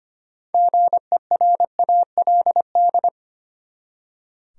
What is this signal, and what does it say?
Morse code, heard as a keyed tone, about 25 wpm, 700 Hz. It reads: GERALD
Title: USA Common Names - 25wpm